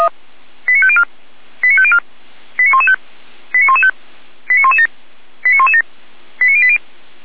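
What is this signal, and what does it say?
Five-tone selective call (ZVEI), heard as quick runs of five tones, tones of about 70 ms. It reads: 89504 89504 89115 89115 89117 89117 89080
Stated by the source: Received with an SDR as ZVEI five-tone selective call, code 89084.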